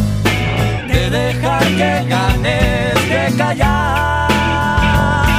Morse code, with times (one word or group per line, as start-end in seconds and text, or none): none